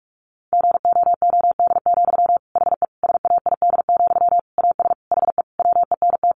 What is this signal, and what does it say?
Morse code, heard as a keyed tone, about 44 wpm, 700 Hz. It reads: GOOD, HE SAID, AS HE WENT
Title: Sentences from Top 600 Words 45wpm